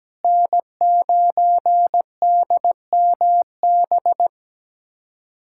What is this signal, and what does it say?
Morse code, heard as a keyed tone, about 17 wpm, 700 Hz. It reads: N9DMB
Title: USA Callsign DE Callsign - 5 Characters - 17wpm